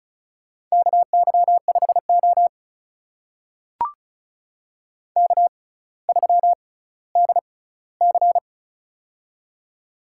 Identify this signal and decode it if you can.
Morse code, heard as a keyed tone, about 36 wpm, spaced as for 18 wpm, 700 Hz. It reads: KY5O K3DC
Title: USA Call Signs - 4 Characters - 6x Character Spacing - 35wpm